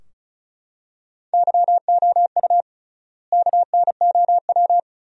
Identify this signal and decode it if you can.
Morse code, heard as a keyed tone, about 36 wpm, 700 Hz. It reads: YOU KNOW
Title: 3x Word-Spacing - Sentences from Top 500 Words - 35wpm